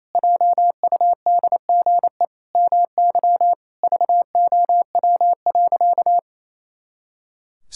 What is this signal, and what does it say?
Morse code, heard as a keyed tone, about 28 wpm, 700 Hz. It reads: JUDGE MY VOW.
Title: Pangram Challenge v2 - 28wpm